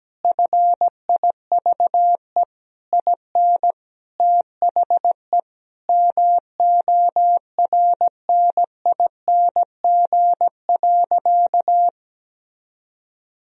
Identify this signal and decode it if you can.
Morse code, heard as a keyed tone, about 17 wpm, 700 Hz. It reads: FIVE IN THE MORNING.
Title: Sentences from Top 1000 Words - Review Part 2 - 17wpm